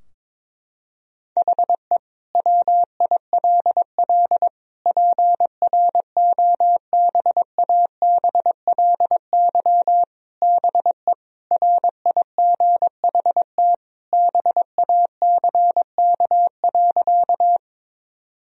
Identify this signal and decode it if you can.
Morse code, heard as a keyed tone, about 22 wpm, 700 Hz. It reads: HE WILL PROBABLY BE RIGHT BACK.
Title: Sentences from Top 1000 Words - Review Part 4 - 22wpm